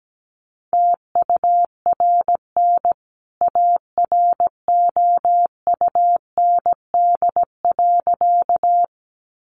Morse code, heard as a keyed tone, about 17 wpm, 700 Hz. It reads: TURN AROUND.
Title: A Sound Of Thunder 17wpm